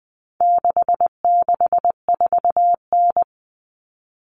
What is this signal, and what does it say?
Morse code, heard as a keyed tone, about 20 wpm, 700 Hz. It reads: 664N
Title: Common Cut Numbers - 4-Digit 20wpm